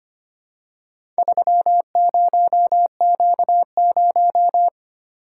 Morse code, heard as a keyed tone, about 25 wpm, 700 Hz. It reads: 30Q0
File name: practice / 25wpm / Four Letter-Number Combinations - 25wpm